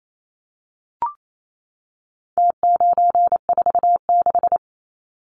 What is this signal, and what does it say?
Morse code, heard as a keyed tone, about 28 wpm, 700 Hz. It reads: T946
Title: Numbers - Contest Style - 4-Digit 28wpm